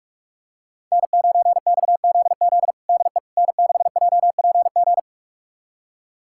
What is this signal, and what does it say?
Morse code, heard as a keyed tone, about 46 wpm, 700 Hz. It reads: N9XZZ DE N6JPG
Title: USA Callsign DE Callsign - 5 Characters - 45wpm